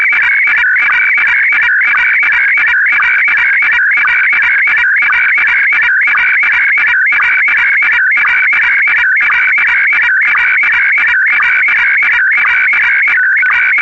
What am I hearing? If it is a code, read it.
SSTV (slow-scan television), Scottie DX.